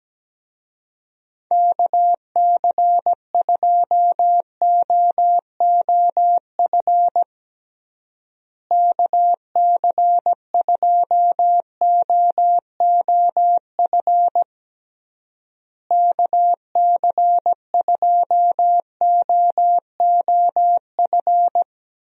Morse code, heard as a keyed tone, about 17 wpm, 700 Hz. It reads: KC2OOF KC2OOF KC2OOF
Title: USA Call Signs - 6 Characters - Repeated 3x - 17wpm